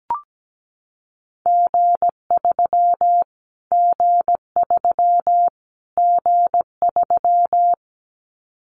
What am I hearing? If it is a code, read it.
Morse code, heard as a keyed tone, about 17 wpm, 700 Hz. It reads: G3 G3 G3
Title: Callsign Prefixes for Top 25 DXCC - Repeated 3x - 17wpm